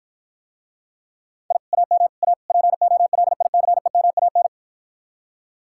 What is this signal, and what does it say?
Morse code, heard as a keyed tone, about 52 wpm, 700 Hz. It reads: I AM A POLICEMAN